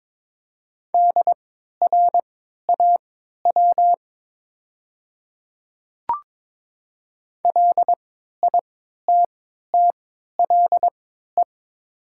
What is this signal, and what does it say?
Morse code, heard as a keyed tone, about 22 wpm, spaced as for 13 wpm, 700 Hz. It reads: DRAW LITTLE
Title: 3x Character Spacing - Top 500 Words - 22wpm